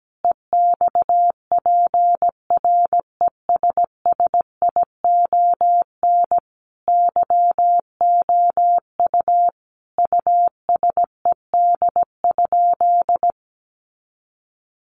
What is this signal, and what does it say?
Morse code, heard as a keyed tone, about 17 wpm, 700 Hz. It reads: EXPRESSION YOU USED?
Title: Sentences from Top 1200 Words 17wpm